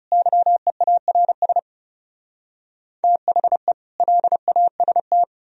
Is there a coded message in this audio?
Yes: Morse code from 0.1 to 5.2 s, heard as a keyed tone, about 40 wpm, 700 Hz. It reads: YEARS THE LAST